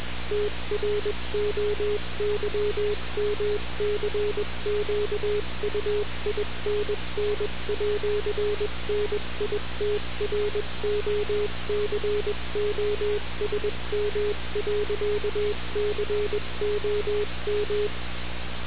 Morse code, heard as a keyed tone, about 21 wpm, 420 Hz, with noise. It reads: TROYMCQUINN@NITROCOSM.COM
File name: radiocap / REC z